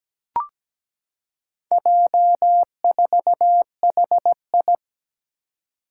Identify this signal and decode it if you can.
Morse code, heard as a keyed tone, about 17 wpm, 700 Hz. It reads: J4HI
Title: Four Challenging Letter-Number Combinations 17wpm